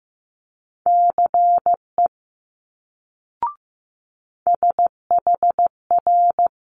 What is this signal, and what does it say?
Morse code, heard as a keyed tone, about 15 wpm, 700 Hz. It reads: CE SHR
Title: Multi-letter Phonemes Quick Fire 15wpm